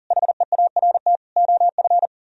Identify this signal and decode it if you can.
Morse code, heard as a keyed tone, about 49 wpm, 700 Hz. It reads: HEART OF
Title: Speed-Racing - Sets of 4 Words 40wpm